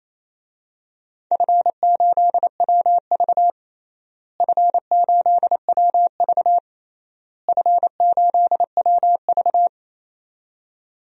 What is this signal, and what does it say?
Morse code, heard as a keyed tone, about 28 wpm, 700 Hz. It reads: F8WV F8WV F8WV
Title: DXCC-100 Call Signs - 4 Characters - Repeated 3x - 28wpm